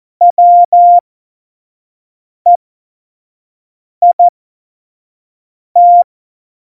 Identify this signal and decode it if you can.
Morse code, heard as a keyed tone, about 14 wpm, spaced as for 5 wpm, 700 Hz. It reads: WEIT
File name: Morsekurs / Witze / Witz 22. August 17, Tempo 25 F70